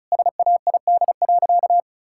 Morse code, heard as a keyed tone, about 36 wpm, 700 Hz. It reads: SAID.
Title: Sentences from Top 1600 Words 35wpm